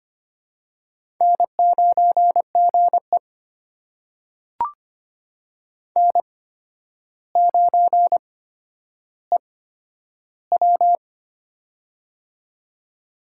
Morse code, heard as a keyed tone, about 25 wpm, spaced as for 11 wpm, 700 Hz. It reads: N9GE N9EW
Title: USA Call Signs - 4 Characters - 8x Character Spacing - 25wpm